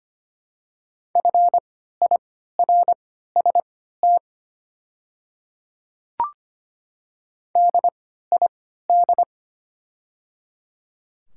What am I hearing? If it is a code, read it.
Morse code, heard as a keyed tone, about 25 wpm, spaced as for 14 wpm, 700 Hz. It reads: FIRST DID
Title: 3x Character Spacing - Top 100 Words - 25wpm